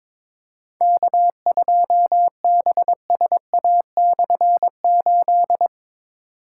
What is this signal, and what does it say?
Morse code, heard as a keyed tone, about 22 wpm, 700 Hz. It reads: K2BSA/8
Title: K1USN SST QSO Practice - 22wpm